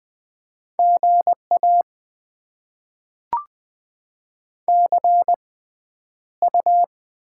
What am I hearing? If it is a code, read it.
Morse code, heard as a keyed tone, about 20 wpm, spaced as for 10 wpm, 700 Hz. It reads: GA CU
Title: Top 20 QSO Elements - 6x Character Spacing - 20wpm